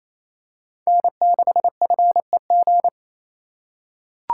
Morse code, heard as a keyed tone, about 28 wpm, 700 Hz. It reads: N6FEG
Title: USA Call Signs - 4x Character Spacing - 28wpm